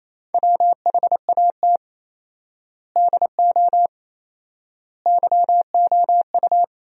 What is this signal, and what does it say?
Morse code, heard as a keyed tone, about 28 wpm, 700 Hz. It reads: WHAT DO YOU
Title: 4x Word-Spacing - Sentences from Top 200 Words - 28wpm